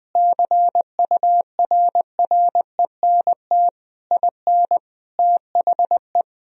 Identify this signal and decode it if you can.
Morse code, heard as a keyed tone, about 21 wpm, 700 Hz. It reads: CURRENT IN THE
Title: Sentences from Top 1000 Words - Review Part 4 - 20wpm